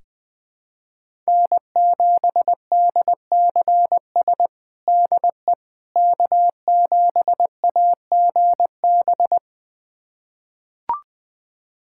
Morse code, heard as a keyed tone, about 20 wpm, 700 Hz. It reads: N7DCS DE K7AGB
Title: USA Callsign DE Callsign 20wpm